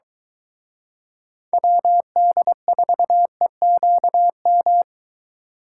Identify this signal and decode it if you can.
Morse code, heard as a keyed tone, about 23 wpm, 700 Hz. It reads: WD4EQM